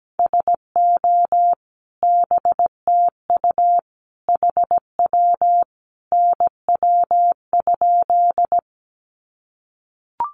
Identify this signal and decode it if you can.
Morse code, heard as a keyed tone, about 17 wpm, 700 Hz. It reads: SO BTU HW NW?
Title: Common QSO Phrases 17wpm